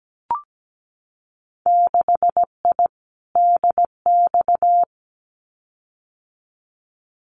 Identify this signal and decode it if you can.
Morse code, heard as a keyed tone, about 17 wpm, 700 Hz. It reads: 6I DX
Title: Winter Field Day Exchanges Encore 17wpm